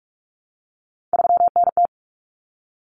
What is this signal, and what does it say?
Morse code, heard as a keyed tone, about 44 wpm, 700 Hz. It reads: 3NT
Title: Common Cut Numbers - 3-Digit 45wpm